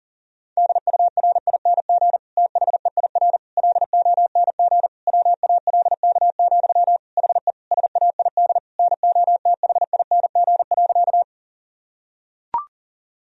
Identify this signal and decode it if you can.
Morse code, heard as a keyed tone, about 40 wpm, 700 Hz. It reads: DURING THEIR LONG WALK, HE SAID NOTHING.
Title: Sentences from Top 1000 Words - Review Part 2 - 40wpm